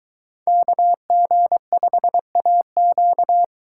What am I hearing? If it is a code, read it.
Morse code, heard as a keyed tone, about 23 wpm, 700 Hz. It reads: KG5AQ